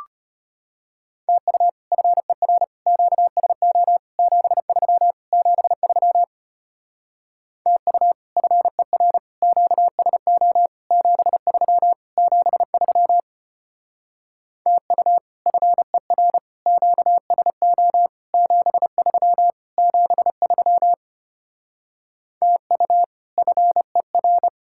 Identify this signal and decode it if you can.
Morse code, heard as a keyed tone, about 33 wpm, 700 Hz. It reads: TU FER QSO 73 73 TU FER QSO 73 73 TU FER QSO 73 73 TU FER